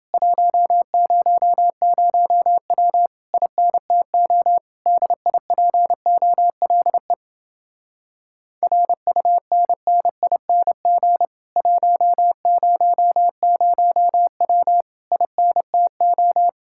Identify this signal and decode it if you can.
Morse code, heard as a keyed tone, about 29 wpm, 700 Hz. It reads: 100W INTO DIPOLE RUNNING 100W INTO